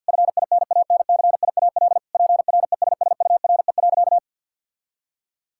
Morse code, heard as a keyed tone, about 49 wpm, 700 Hz. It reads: FINANCIAL PRESSURE.
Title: Sentences from Top 1000 Words - Review Part 6 - 50wpm